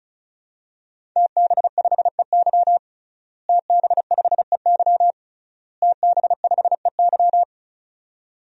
Morse code, heard as a keyed tone, about 36 wpm, 700 Hz. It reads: TB5EY TB5EY TB5EY
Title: DXCC-100 Call Signs - 5 Characters - Repeated 3x - 35wpm